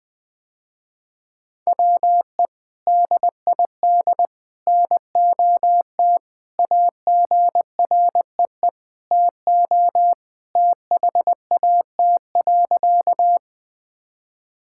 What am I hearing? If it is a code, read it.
Morse code, heard as a keyed tone, about 20 wpm, 700 Hz. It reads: WE DID NOT AGREE TO THAT.
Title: Sentences with 3-Word Phrases - Spoken Beginning - 20wpm